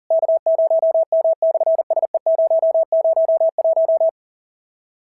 Morse code, heard as a keyed tone, about 40 wpm, 650 Hz. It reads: K0M/SE001
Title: SOTA QSO Practice 40wpm